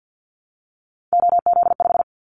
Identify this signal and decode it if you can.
Morse code, heard as a keyed tone, about 50 wpm, 700 Hz. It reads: OZ5
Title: Callsign Prefixes for Top 50 DXCC - 50wpm